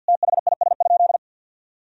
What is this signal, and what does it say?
Morse code, heard as a keyed tone, about 52 wpm, 700 Hz. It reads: THIS?
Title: Sentences from Top 1000 Words - Review Part 2 - 50wpm